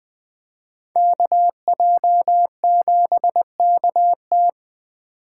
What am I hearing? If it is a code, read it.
Morse code, heard as a keyed tone, about 21 wpm, 700 Hz. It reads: KJ7KT